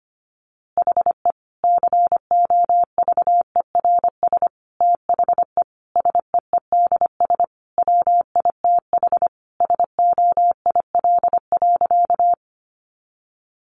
Morse code, heard as a keyed tone, about 25 wpm, 700 Hz. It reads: HE COVERS THE SEEDS WITH SOIL.
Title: Easy Phrases 25wpm